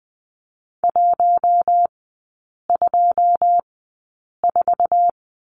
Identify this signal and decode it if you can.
Morse code, heard as a keyed tone, about 20 wpm, spaced as for 8 wpm, 700 Hz. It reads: 124
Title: 121 Lesson 20e - Number 3 - Three Characters 20wpm